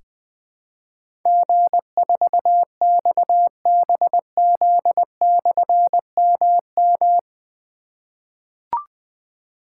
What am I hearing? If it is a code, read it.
Morse code, heard as a keyed tone, about 20 wpm, 700 Hz. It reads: G4XBZ/MM
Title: Difficult Call Signs 20wpm